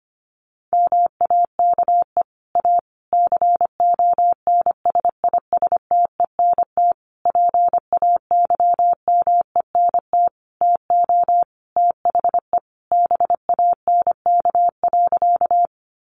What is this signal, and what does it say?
Morse code, heard as a keyed tone, about 25 wpm, 700 Hz. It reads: MAKE A CONSISTENT PAYMENT TO THE BANK.
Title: Sentences from Top 1500 Words 25wpm